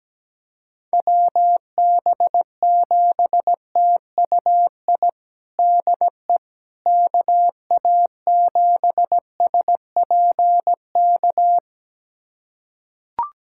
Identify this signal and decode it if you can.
Morse code, heard as a keyed tone, about 17 wpm, 700 Hz. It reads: WB7TUI DE KA7SPK
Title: USA Callsign DE Callsign 17wpm